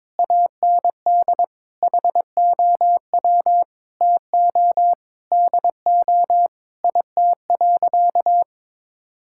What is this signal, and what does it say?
Morse code, heard as a keyed tone, about 22 wpm, 700 Hz. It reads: AND HOW TO DO IT.
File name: practice / 22wpm / Sentences with 8-Word Phrases - 22wpm